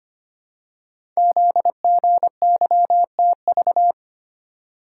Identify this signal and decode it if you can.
Morse code, heard as a keyed tone, about 25 wpm, 700 Hz. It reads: ZGYTV
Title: Code Groups - 5 Letters - 25wpm